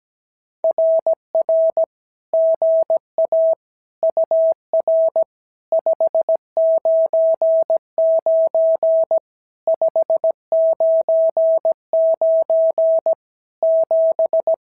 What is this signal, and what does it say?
Morse code, heard as a keyed tone, about 17 wpm, 650 Hz. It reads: RR GA UR 599 599 7
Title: SOTA QSO Practice 17wpm